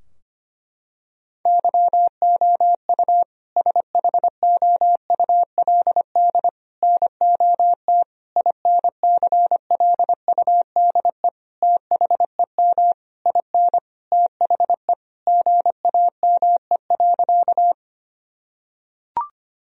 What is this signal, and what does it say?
Morse code, heard as a keyed tone, about 25 wpm, 700 Hz. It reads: YOU SHOULD NOT INCLUDE THEM IN THE GAME.
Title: Sentences from Top 1000 Words - Review Part 3 - 25wpm